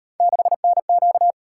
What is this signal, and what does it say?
Morse code, heard as a keyed tone, about 37 wpm, 700 Hz. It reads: 6NQ